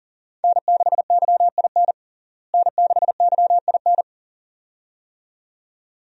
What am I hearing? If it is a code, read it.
Morse code, heard as a keyed tone, about 37 wpm, 700 Hz. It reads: N6YIN N6YIN
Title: USA Call Signs - 5 Characters - Repeated 2x - 40wpm